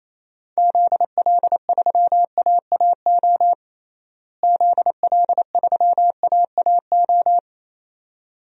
Morse code, heard as a keyed tone, about 28 wpm, 700 Hz. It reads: ZL3AAO ZL3AAO
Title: DXCC-100 Call Signs - 6 Characters - Repeated 2x - 28wpm